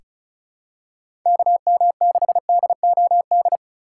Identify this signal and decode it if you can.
Morse code, heard as a keyed tone, about 34 wpm, 700 Hz. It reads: KM6DOD
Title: USA Call Signs - 6 Characters - 3x Character Spacing - 35wpm